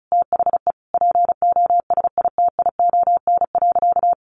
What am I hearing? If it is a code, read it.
Morse code, heard as a keyed tone, about 34 wpm, 700 Hz. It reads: THE POSITION.
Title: Sentences from Top 1400 Words 35wpm